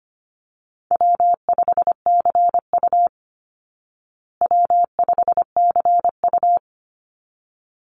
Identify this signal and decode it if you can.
Morse code, heard as a keyed tone, about 25 wpm, 700 Hz. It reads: W5CU W5CU
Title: Easy Call Signs Repeated 2x - 25wpm